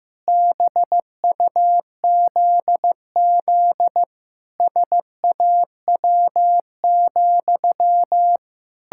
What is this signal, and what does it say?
Morse code, heard as a keyed tone, about 15 wpm, 700 Hz. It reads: BUZZ SAW,
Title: Pangram Challenge v2 - 15wpm